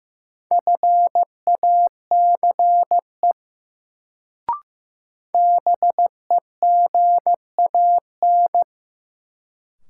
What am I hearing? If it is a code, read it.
Morse code, heard as a keyed tone, about 15 wpm, 700 Hz. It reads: FACE BEGAN
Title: Sentences from Top 300 Words - New Word Drill - 15wpm